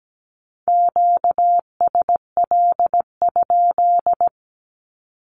Morse code, heard as a keyed tone, about 17 wpm, 700 Hz. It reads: QSL?
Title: Q-Codes 17wpm